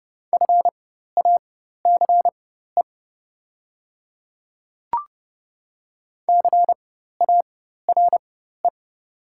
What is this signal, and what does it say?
Morse code, heard as a keyed tone, about 30 wpm, spaced as for 14 wpm, 700 Hz. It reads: FACE CARE
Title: 4x Character Spacing - Top 300 Words - 30wpm